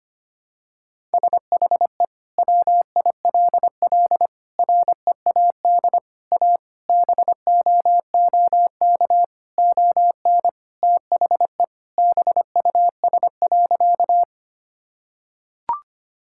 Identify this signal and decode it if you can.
Morse code, heard as a keyed tone, about 25 wpm, 700 Hz. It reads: SHE WILL READ A BOOK ON THE BUS.
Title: Sentences from Top 1000 Words - Review Part 6 - 25wpm